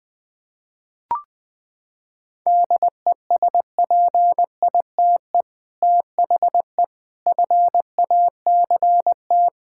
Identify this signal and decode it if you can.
Morse code, heard as a keyed tone, about 20 wpm, 700 Hz. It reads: DESPITE THE FACT
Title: Sentences with 4-Word Phrases - 20wpm